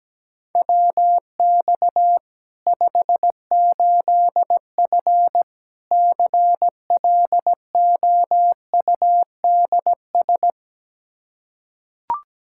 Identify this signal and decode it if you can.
Morse code, heard as a keyed tone, about 17 wpm, 700 Hz. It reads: WX 58F CLOUDS
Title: Practice QSO - Exchange 2 - Info - 17wpm